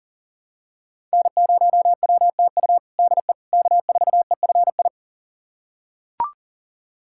Morse code, heard as a keyed tone, about 40 wpm, 700 Hz. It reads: N0WTU DE K4EFI
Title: USA Callsign DE Callsign - 5 Characters - 40wpm